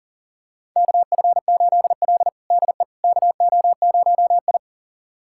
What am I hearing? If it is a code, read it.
Morse code, heard as a keyed tone, about 40 wpm, 700 Hz. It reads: KF8L DE KO0I
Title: USA Callsign DE Callsign - 4 Characters - 40wpm